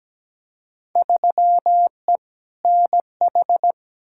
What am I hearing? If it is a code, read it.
Morse code, heard as a keyed tone, about 17 wpm, 700 Hz. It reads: 3E NH
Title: Field Day Exchanges Encore - 17wpm